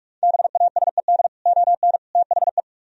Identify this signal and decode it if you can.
Morse code, heard as a keyed tone, about 47 wpm, 700 Hz. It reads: BASED ON THE